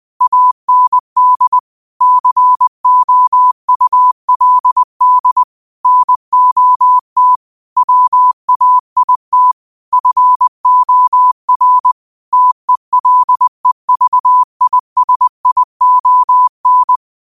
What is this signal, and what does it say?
Morse code, heard as a keyed tone, about 20 wpm, 1000 Hz. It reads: AND COULD NOT WAIT FOR TELEVISION
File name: qotdfiles / 20wpm